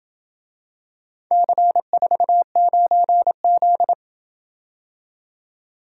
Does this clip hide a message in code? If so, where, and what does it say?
Yes: Morse code from 1.3 to 3.9 s, heard as a keyed tone, about 27 wpm, 700 Hz. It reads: C49Z